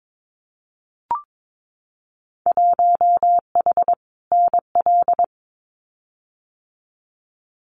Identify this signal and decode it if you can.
Morse code, heard as a keyed tone, about 22 wpm, 700 Hz. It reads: 1H NL
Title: Winter Field Day Exchanges 22wpm